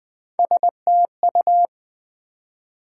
Morse code, heard as a keyed tone, about 20 wpm, 700 Hz. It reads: STU
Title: Build A Word - 1 Letter At A Time - 4-Letter Words - 20wpm